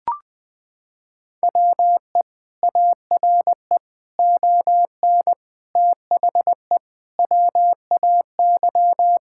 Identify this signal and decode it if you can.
Morse code, heard as a keyed tone, about 20 wpm, 700 Hz. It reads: WE ARE ON THE WAY
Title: Sentences with 4-Word Phrases - 20wpm